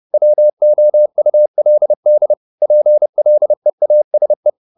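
Morse code, heard as a keyed tone, about 30 wpm, 600 Hz. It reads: WOULD PLEASE
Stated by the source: Morse code at 30 wpm, the novel Never by Ken Follett in English, from CW med Gnister.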